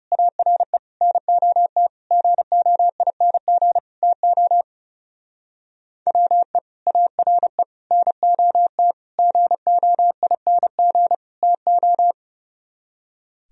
Morse code, heard as a keyed tone, about 38 wpm, 700 Hz. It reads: ARE NOT GOING TO WE ARE NOT GOING TO